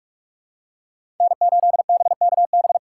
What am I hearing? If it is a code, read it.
Morse code, heard as a keyed tone, about 46 wpm, 700 Hz. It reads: N8BKB
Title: USA Call Signs - 5 Characters - 2x Character Spacing - 45wpm